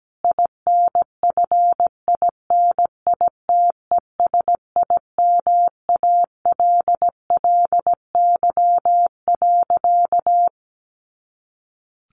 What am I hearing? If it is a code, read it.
Morse code, heard as a keyed tone, about 17 wpm, 700 Hz. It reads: INFINITESIMALLY.
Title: A Sound Of Thunder 17wpm